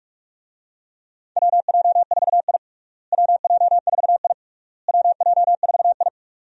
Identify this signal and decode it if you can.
Morse code, heard as a keyed tone, about 41 wpm, 700 Hz. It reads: WJ4I WJ4I WJ4I
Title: USA Call Signs - 4 Characters - Repeated 3x - 45wpm